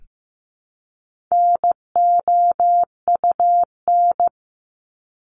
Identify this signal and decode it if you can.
Morse code, heard as a keyed tone, about 15 wpm, 700 Hz. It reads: NOUN
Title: Top 500 Words Repeated 15wpm